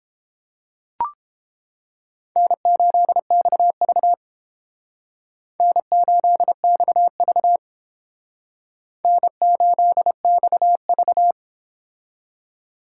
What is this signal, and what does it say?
Morse code, heard as a keyed tone, about 28 wpm, 700 Hz. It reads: N8XV N8XV N8XV